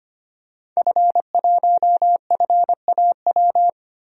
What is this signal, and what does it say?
Morse code, heard as a keyed tone, about 25 wpm, 700 Hz. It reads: F1FAW
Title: DXCC-100 Call Signs - 5 Characters - Repeated 3x - 25wpm